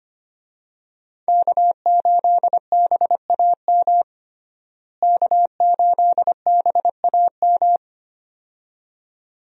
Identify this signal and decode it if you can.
Morse code, heard as a keyed tone, about 25 wpm, 700 Hz. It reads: K8BAM K8BAM
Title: USA Call Signs - 5 Characters - Repeated 2x - 25wpm